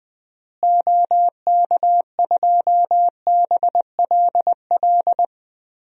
Morse code, heard as a keyed tone, about 20 wpm, 700 Hz. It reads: OK2BLL
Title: DXCC-100 Call Signs - 6 Characters - Repeated 3x - 20wpm